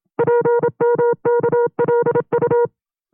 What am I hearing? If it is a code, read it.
Morse code, heard as a keyed tone, about 28 wpm, 480 Hz. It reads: PMKLU
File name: FreeFiles / CW-practice / 25WPM-side1